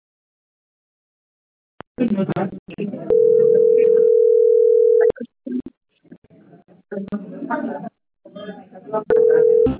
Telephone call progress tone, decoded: ringback tone

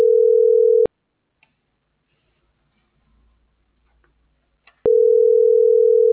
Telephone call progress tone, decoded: ringback tone

